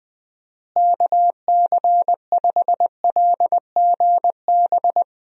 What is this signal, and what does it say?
Morse code, heard as a keyed tone, about 20 wpm, 700 Hz. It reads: KC5LGB
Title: USA Call Signs - 6 Characters - Repeated 3x - 20wpm